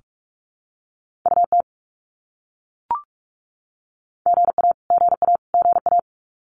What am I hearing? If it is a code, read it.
Morse code, heard as a keyed tone, about 44 wpm, 700 Hz. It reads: UT GA GA GA
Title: US States Repeated 3x 45wpm